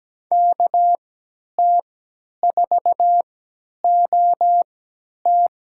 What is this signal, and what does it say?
Morse code, heard as a keyed tone, about 17 wpm, spaced as for 10 wpm, 700 Hz. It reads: KT4OT
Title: USA Call Signs - 5 Characters - 3x Character Spacing - 17wpm